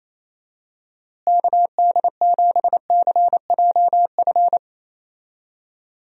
Morse code, heard as a keyed tone, about 28 wpm, 700 Hz. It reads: KD7CJF
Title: USA Call Signs - 6 Characters - Repeated 3x - 28wpm